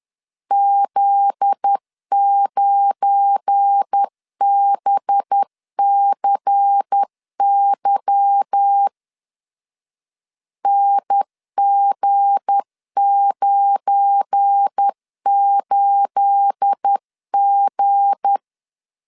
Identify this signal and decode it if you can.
Morse code, heard as a keyed tone, about 10 wpm, 800 Hz. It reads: Z9BCY NG98G
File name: LEZIONI / 9 28 10wpm 2842